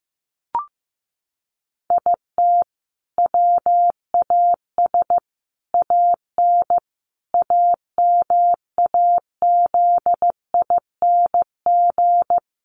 Morse code, heard as a keyed tone, about 15 wpm, 700 Hz. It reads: IT WAS AN AMAZING
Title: Sentences from Top 1500 Words 15wpm